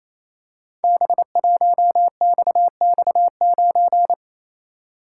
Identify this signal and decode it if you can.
Morse code, heard as a keyed tone, about 29 wpm, 700 Hz. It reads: B1XX9
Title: Code Groups - 5 Letter:Numbers - 28wpm